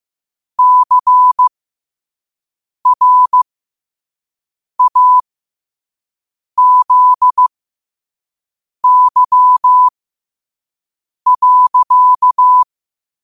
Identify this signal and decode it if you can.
Morse code, heard as a keyed tone, about 15 wpm, spaced as for 5 wpm, 1000 Hz. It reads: CRAZY.